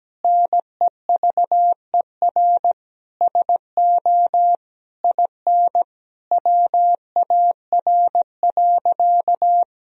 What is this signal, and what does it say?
Morse code, heard as a keyed tone, about 17 wpm, 700 Hz. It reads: NEVER SO IN WAR.
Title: Sentences from Top 1000 Words - Review Part 2 - 17wpm